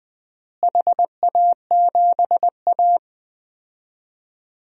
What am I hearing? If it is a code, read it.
Morse code, heard as a keyed tone, about 19 wpm, 700 Hz. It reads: HA7A
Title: DXCC-100 Call Signs - 4 Characters - Repeated 3x - 20wpm